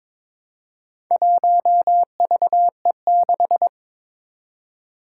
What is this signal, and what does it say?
Morse code, heard as a keyed tone, about 22 wpm, 700 Hz. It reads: 1VE6
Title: Four Challenging Letter-Number Combinations 22wpm